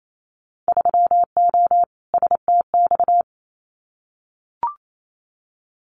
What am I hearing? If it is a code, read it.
Morse code, heard as a keyed tone, about 28 wpm, 700 Hz. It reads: 3O STX
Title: Winter Field Day Exchanges 28wpm